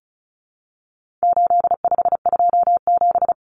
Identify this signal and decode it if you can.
Morse code, heard as a keyed tone, about 34 wpm, 700 Hz. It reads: 8527
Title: Common Cut Numbers - 4-Digit 35wpm